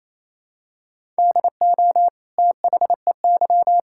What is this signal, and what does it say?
Morse code, heard as a keyed tone, about 28 wpm, 700 Hz. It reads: DO THEY
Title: Sentences from Top 1600 Words 28wpm